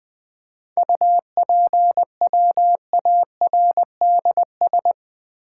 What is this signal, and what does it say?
Morse code, heard as a keyed tone, about 20 wpm, 700 Hz. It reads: UPWARDS
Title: 7 Letter Words - First 3 Letters Given - 20wpm